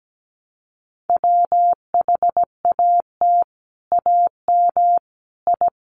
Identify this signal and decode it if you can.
Morse code, heard as a keyed tone, about 17 wpm, 700 Hz. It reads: WHAT AM I
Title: Sets of 5 Words Encore2 17wpm